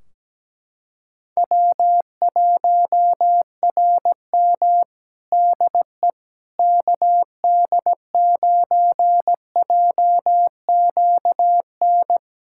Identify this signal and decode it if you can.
Morse code, heard as a keyed tone, about 17 wpm, 700 Hz. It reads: W1RM DE KD9JQN
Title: K1USN SST QSO Practice 17wpm